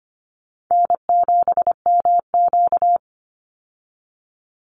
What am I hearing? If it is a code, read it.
Morse code, heard as a keyed tone, about 25 wpm, 700 Hz. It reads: N7MQ
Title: Call Signs Repeated 3x - 25wpm